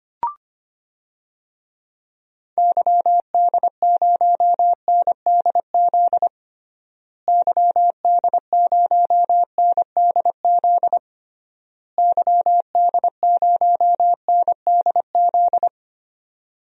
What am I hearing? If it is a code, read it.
Morse code, heard as a keyed tone, about 25 wpm, 700 Hz. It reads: YD0NDZ YD0NDZ YD0NDZ
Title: DXCC-100 Call Signs - 6 Characters - Repeated 3x - 25wpm